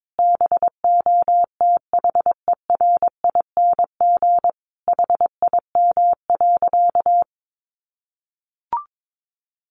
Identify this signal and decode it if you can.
Morse code, heard as a keyed tone, about 22 wpm, 700 Hz. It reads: BOTHERING HIM.
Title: Additional Easy Phrases 22wpm